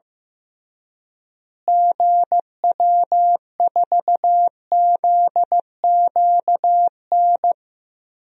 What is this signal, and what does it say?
Morse code, heard as a keyed tone, about 15 wpm, 700 Hz. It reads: GW4ZQN
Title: DXCC-100 Call Signs - 6 Characters - Repeated 3x - 15wpm